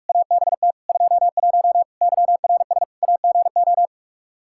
Morse code, heard as a keyed tone, about 42 wpm, 700 Hz. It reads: ABT 21 YRS AGO
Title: Practice QSO - Exchange 3 - More Info - 45wpm